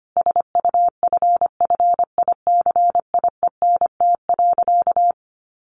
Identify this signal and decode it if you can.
Morse code, heard as a keyed tone, about 25 wpm, 700 Hz. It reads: SUFFICIENT.
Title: Sentences from Top 1400 Words 25wpm